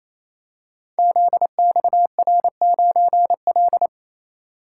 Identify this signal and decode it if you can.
Morse code, heard as a keyed tone, about 28 wpm, 700 Hz. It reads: ZXR9L
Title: Code Groups - 5 Letter:Numbers - 28wpm